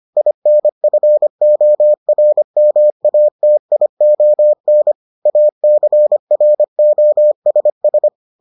Morse code, heard as a keyed tone, about 25 wpm, 600 Hz. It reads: INFORMATION ACROSS